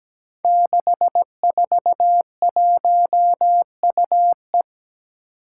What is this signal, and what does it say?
Morse code, heard as a keyed tone, about 17 wpm, 700 Hz. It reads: 641UE
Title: Five Challenging Letter-Number Combinations 17wpm